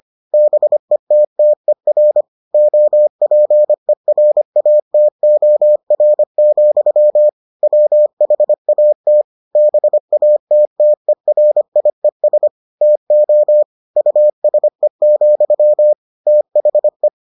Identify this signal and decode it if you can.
Morse code, heard as a keyed tone, about 25 wpm, 600 Hz. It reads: BETTER OPERATOR, WHAT BATTERIES TO USE, THE